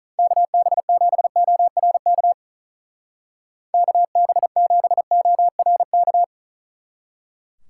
Morse code, heard as a keyed tone, about 44 wpm, 700 Hz. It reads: KB7ORK KB7ORK